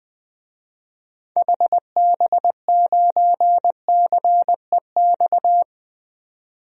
Morse code, heard as a keyed tone, about 20 wpm, 700 Hz. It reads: HB9CEX
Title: DXCC-100 Call Signs - 6 Characters - 20wpm